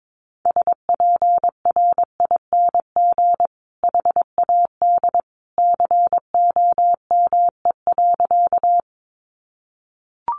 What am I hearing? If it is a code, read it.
Morse code, heard as a keyed tone, about 22 wpm, 700 Hz. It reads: SPRING HAD COME.
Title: Sentences from Top 1200 Words 22wpm